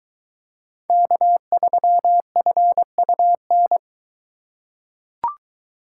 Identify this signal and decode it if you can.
Morse code, heard as a keyed tone, about 22 wpm, 700 Hz. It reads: K3FUN